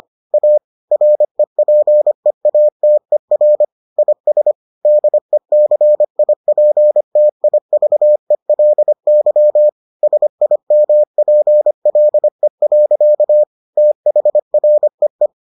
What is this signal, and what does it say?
Morse code, heard as a keyed tone, about 25 wpm, 600 Hz. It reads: A REPEATER IS DECIPTIVELY SIMPLE. THREE